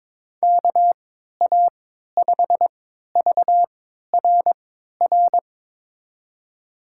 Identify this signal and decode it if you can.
Morse code, heard as a keyed tone, about 21 wpm, spaced as for 12 wpm, 700 Hz. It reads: KA5VRR
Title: USA Call Signs - 6 Characters - 3x Character Spacing - 22wpm